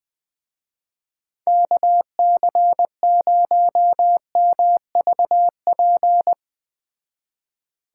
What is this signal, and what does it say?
Morse code, heard as a keyed tone, about 20 wpm, 700 Hz. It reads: KC0MVP
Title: Speed Racing - USA Call Signs 17wpm